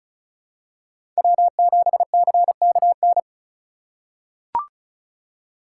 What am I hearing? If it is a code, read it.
Morse code, heard as a keyed tone, about 36 wpm, 700 Hz. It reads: W7CKN
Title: USA Call Signs - 4x Character Spacing - 35wpm